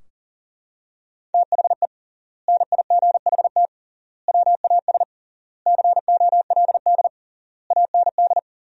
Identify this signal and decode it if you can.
Morse code, heard as a keyed tone, about 40 wpm, 700 Hz. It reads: THE NIGHT WAS COLD AND
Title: 3x Word-Spacing - Sentences from Top 400 Words - 40wpm